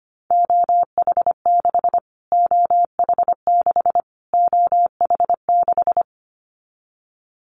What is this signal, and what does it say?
Morse code, heard as a keyed tone, about 25 wpm, 700 Hz. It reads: OH6 OH6 OH6
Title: Callsign Prefixes for Top 25 DXCC - Repeated 3x - 25wpm